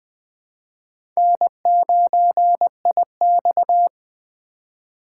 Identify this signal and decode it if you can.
Morse code, heard as a keyed tone, about 20 wpm, 700 Hz. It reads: N9IX
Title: USA Call Signs - 4 Characters - 2x Character Spacing - 20wpm